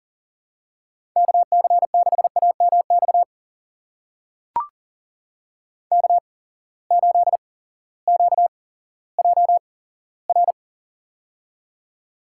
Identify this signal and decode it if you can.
Morse code, heard as a keyed tone, about 41 wpm, spaced as for 18 wpm, 700 Hz. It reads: KC6AMX K8QJR
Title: USA Call Signs - 8x Character Spacing - 40wpm